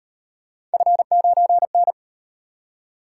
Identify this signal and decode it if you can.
Morse code, heard as a keyed tone, about 36 wpm, 700 Hz. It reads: F9N